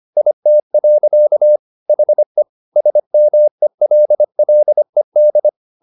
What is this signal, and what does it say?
Morse code, heard as a keyed tone, about 25 wpm, 600 Hz. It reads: IT. HE SMELLED
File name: the old man and the sea part 1 25wpm